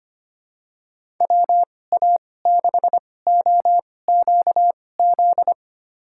Morse code, heard as a keyed tone, about 25 wpm, spaced as for 18 wpm, 700 Hz. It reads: WA6OQZ
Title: USA Call Signs - 6 Characters - 2x Character Spacing - 25wpm